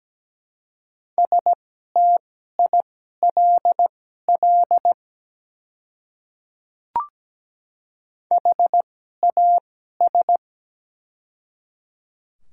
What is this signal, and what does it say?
Morse code, heard as a keyed tone, about 17 wpm, spaced as for 12 wpm, 700 Hz. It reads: STILL HAS
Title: 2x Character Spacing - Top 300 Words - 17wpm